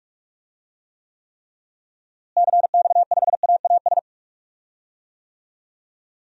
Morse code, heard as a keyed tone, about 41 wpm, 700 Hz. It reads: CX5AAS
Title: DXCC-100 Call Signs - 6 Characters - Repeated 3x - 45wpm